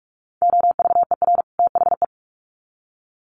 Morse code, heard as a keyed tone, about 44 wpm, 700 Hz. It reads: OVER THE
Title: Sets of 2 Words 45wpm